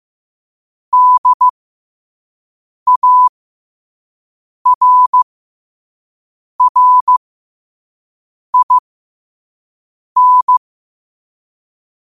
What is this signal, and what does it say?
Morse code, heard as a keyed tone, about 15 wpm, spaced as for 5 wpm, 1000 Hz. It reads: DARRIN
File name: qotdfiles / 5wpm